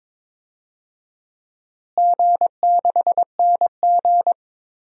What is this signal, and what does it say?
Morse code, heard as a keyed tone, about 22 wpm, 700 Hz. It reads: G6NG
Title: DXCC-100 Call Signs - 4 Characters - Repeated 2x - 22wpm